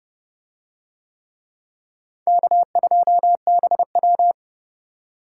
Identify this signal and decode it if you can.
Morse code, heard as a keyed tone, about 28 wpm, 700 Hz. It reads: K2BW